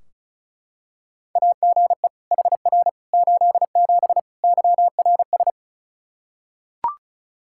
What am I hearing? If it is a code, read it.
Morse code, heard as a keyed tone, about 34 wpm, 700 Hz. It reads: AGE HR 87 YRS
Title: Practice QSO - Exchange 3 - More Info - 35wpm